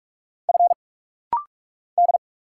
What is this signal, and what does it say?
Morse code, heard as a keyed tone, about 45 wpm, 700 Hz. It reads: F D
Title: Single Letters - Rapid-Fire 45wpm